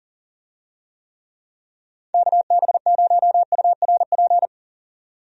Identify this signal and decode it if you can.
Morse code, heard as a keyed tone, about 37 wpm, 700 Hz. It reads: KB0URP
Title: USA Call Signs - 6 Characters - 40wpm